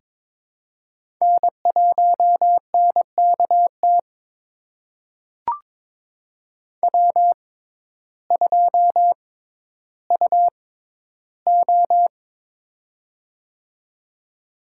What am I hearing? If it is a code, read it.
Morse code, heard as a keyed tone, about 22 wpm, spaced as for 12 wpm, 700 Hz. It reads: N1NKT W2UO
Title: USA Call Signs - 6x Character Spacing - 22wpm